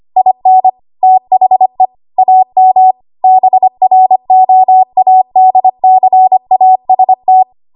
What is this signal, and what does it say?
Morse code, heard as a keyed tone, about 25 wpm, 750 Hz. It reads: IN THE AM BROADCAST
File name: Archive / 25 WPM / 25WPM